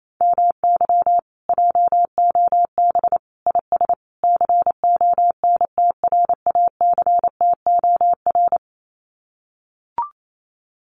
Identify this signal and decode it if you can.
Morse code, heard as a keyed tone, about 28 wpm, 700 Hz. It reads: MY JOB IS CONTRACTOR
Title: Common QSO Phrases Encore 28wpm